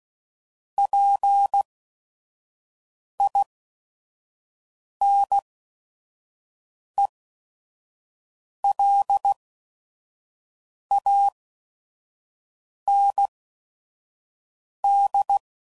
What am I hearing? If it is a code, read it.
Morse code, heard as a keyed tone, about 16 wpm, spaced as for 5 wpm, 800 Hz. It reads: PINELAND
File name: CW Code Groups @5WPM / fromSM's / Cities